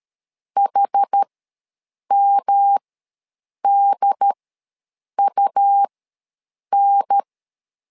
Morse code, heard as a keyed tone, about 13 wpm, spaced as for 7 wpm, 800 Hz. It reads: HMDUN
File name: LEZIONI / 3 MWRO / 3 M ws 12wpm 1745